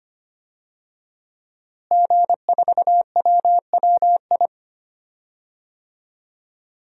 Morse code, heard as a keyed tone, about 25 wpm, 700 Hz. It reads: G4WWI
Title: DXCC-100 Call Signs - 5 Characters - Repeated 3x - 25wpm